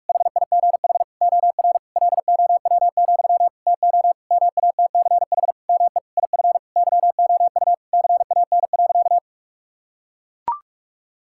Morse code, heard as a keyed tone, about 43 wpm, 700 Hz. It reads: HIGH OR LOW, TO MATCH ME IF YOU CAN.
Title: Sentences from Top 1600 Words 45wpm